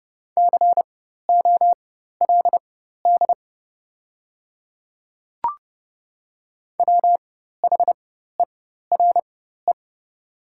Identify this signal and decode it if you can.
Morse code, heard as a keyed tone, about 30 wpm, spaced as for 14 wpm, 690 Hz. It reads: COLD WHERE
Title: 4x Character Spacing - Top 500 Words - 30wpm